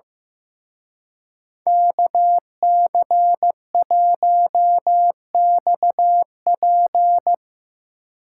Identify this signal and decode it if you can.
Morse code, heard as a keyed tone, about 17 wpm, 700 Hz. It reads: KC1XP
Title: Speed Racing - USA 2x2 Call Signs 15wpm